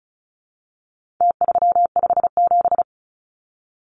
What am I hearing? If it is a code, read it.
Morse code, heard as a keyed tone, about 34 wpm, 700 Hz. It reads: T357
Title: Numbers - Contest Style - 4-Digit 35wpm